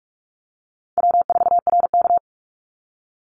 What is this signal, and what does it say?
Morse code, heard as a keyed tone, about 44 wpm, 700 Hz. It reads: W4RK
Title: NA SOTA Chaser Callsigns 45wpm